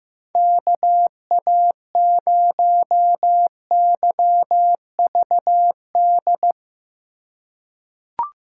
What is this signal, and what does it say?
Morse code, heard as a keyed tone, about 15 wpm, 700 Hz. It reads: KA0YVD
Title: USA Call Signs - 3x Character Spacing - 15wpm